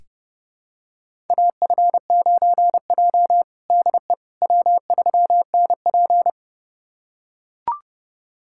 Morse code, heard as a keyed tone, about 30 wpm, 700 Hz. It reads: AF9J DE W3NP
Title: USA Callsign DE Callsign - 4 Characters - 30wpm